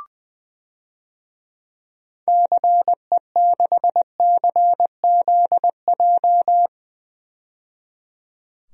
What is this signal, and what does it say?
Morse code, heard as a keyed tone, about 21 wpm, 700 Hz. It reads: CE6CZJ
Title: DXCC-100 Call Signs - 6 Characters - 20wpm